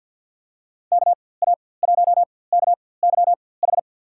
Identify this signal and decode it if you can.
Morse code, heard as a keyed tone, about 47 wpm, spaced as for 23 wpm, 700 Hz. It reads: KA1KYH
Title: USA Call Signs - 6 Characters - 4x Character Spacing - 50wpm